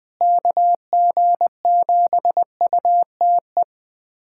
Morse code, heard as a keyed tone, about 20 wpm, 700 Hz. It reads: KG7UTE
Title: USA Callsign DE Callsign - 6 Characters - 20wpm